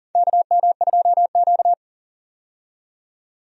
Morse code, heard as a keyed tone, about 40 wpm, 700 Hz. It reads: KM2Q
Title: USA Call Signs - 4 Characters - 40wpm